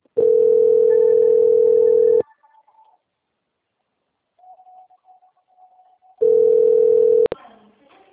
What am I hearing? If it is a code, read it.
Ringback tone (North America), 440 + 480 Hz.
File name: out- - - - - .22685